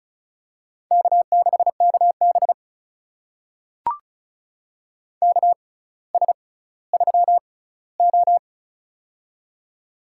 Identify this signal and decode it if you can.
Morse code, heard as a keyed tone, about 34 wpm, spaced as for 18 wpm, 700 Hz. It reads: K6KB KS3O
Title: USA Call Signs - 4 Characters - 6x Character Spacing - 35wpm